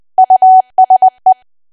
Morse code, heard as a keyed tone, about 20 wpm, 750 Hz. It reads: USE
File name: cwa / W1AW-Level3 / 20WPM